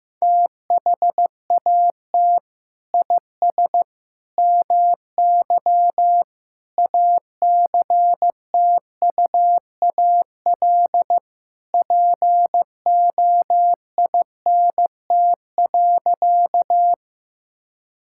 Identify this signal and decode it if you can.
Morse code, heard as a keyed tone, about 15 wpm, 700 Hz. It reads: THAT IS MY ACTUAL POINT.
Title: Sentences from Top 1000 Words - Review Part 5 - 15wpm